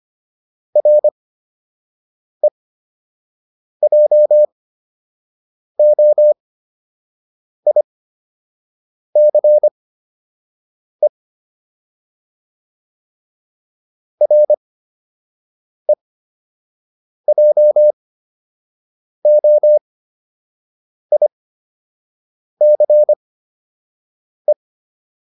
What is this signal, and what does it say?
Morse code, heard as a keyed tone, about 25 wpm, spaced as for 6 wpm, 600 Hz. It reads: REJOICE REJOICE